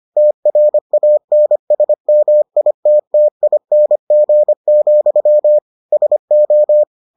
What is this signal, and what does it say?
Morse code, heard as a keyed tone, about 25 wpm, 600 Hz. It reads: TRANSMITTING, SO